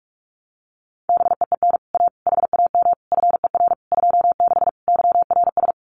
Morse code, heard as a keyed tone, about 44 wpm, 700 Hz. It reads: BEEN A HAM FER 26 YRS
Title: Common QSO Phrases Encore 45wpm